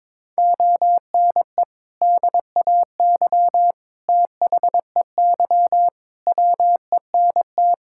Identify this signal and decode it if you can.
Morse code, heard as a keyed tone, about 22 wpm, 700 Hz. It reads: ONE DAY THEY WENT
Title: Sentences from Top 1600 Words 22wpm